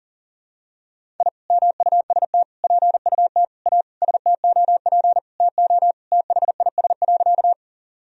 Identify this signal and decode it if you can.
Morse code, heard as a keyed tone, about 37 wpm, 700 Hz. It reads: I MUST PUT A STOP TO THIS.
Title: Sentences from Top 1000 Words - Review Part 1 - 40wpm